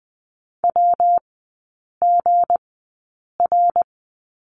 Morse code, heard as a keyed tone, about 20 wpm, spaced as for 8 wpm, 700 Hz. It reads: WGR
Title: 191 Lesson 31e - Letter v - Three Characters 20wpm